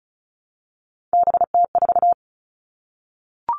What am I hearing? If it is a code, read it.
Morse code, heard as a keyed tone, about 34 wpm, 700 Hz. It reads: BT4
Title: All Cut Numbers - 3-Digit 35wpm